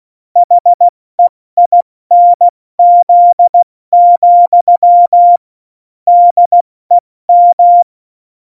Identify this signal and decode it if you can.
Morse code, heard as a keyed tone, about 16 wpm, 700 Hz. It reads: HEINZ, DEM
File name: Morsekurs / Witze / Witz 18. Februar 2020, 70 F80